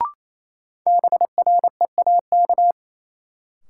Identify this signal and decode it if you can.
Morse code, heard as a keyed tone, about 28 wpm, 700 Hz. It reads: BREAK
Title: Sets of 4 Words - New Word Drill - 28wpm